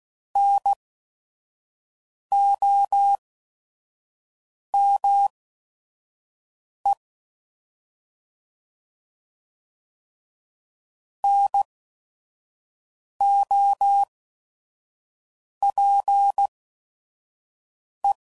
Morse code, heard as a keyed tone, about 16 wpm, spaced as for 5 wpm, 800 Hz. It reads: NOME NOPE